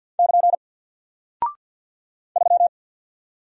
Morse code, heard as a keyed tone, about 49 wpm, 700 Hz. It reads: <KN> 3
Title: Single Character (Letter:Number:Punctuation:Prosign) Alt 50wpm